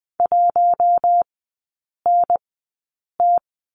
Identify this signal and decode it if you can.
Morse code, heard as a keyed tone, about 20 wpm, spaced as for 8 wpm, 700 Hz. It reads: 1NT
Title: 087 Lesson 15e - Number 5 - Three Characters 20wpm